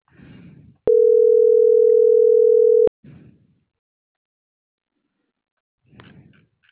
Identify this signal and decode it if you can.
Ringback tone (North America), 440 + 480 Hz.